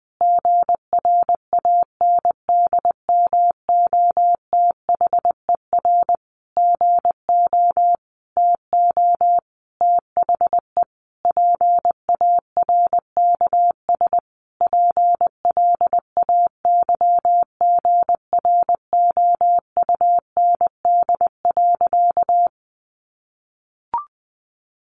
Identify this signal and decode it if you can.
Morse code, heard as a keyed tone, about 20 wpm, 700 Hz. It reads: GRANDMOTHER GO TO THE PARKS PLAYGROUND.
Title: Additional Easy Phrases 20wpm